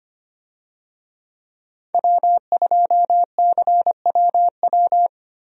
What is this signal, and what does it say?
Morse code, heard as a keyed tone, about 25 wpm, 700 Hz. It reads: W2CWW
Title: USA Call Signs - 5 Characters - 25wpm